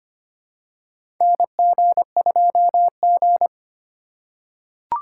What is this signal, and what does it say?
Morse code, heard as a keyed tone, about 25 wpm, 700 Hz. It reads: NG2G
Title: USA Call Signs - 4 Characters - 8x Character Spacing - 25wpm